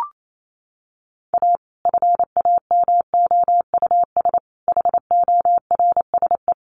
Morse code, heard as a keyed tone, about 28 wpm, 700 Hz. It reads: A FAMOUS HORSE
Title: Sentences from Top 1500 Words 28wpm